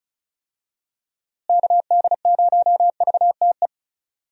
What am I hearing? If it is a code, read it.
Morse code, heard as a keyed tone, about 34 wpm, 700 Hz. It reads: KD0VTE
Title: USA Call Signs - 6 Characters - Repeated 3x - 35wpm